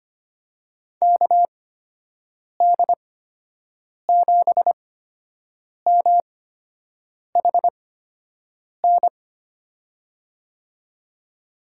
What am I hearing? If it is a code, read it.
Morse code, heard as a keyed tone, about 25 wpm, spaced as for 7 wpm, 700 Hz. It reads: KD7MHN
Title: USA Call Signs - 6 Characters - 8x Character Spacing - 25wpm